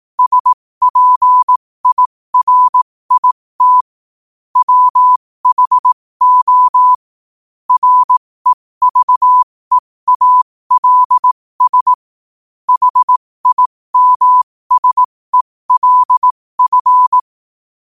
Morse code, heard as a keyed tone, about 18 wpm, spaced as for 15 wpm, 1000 Hz. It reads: SPIRIT WHO REVEALS HIMSELF
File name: qotdfiles / 15wpm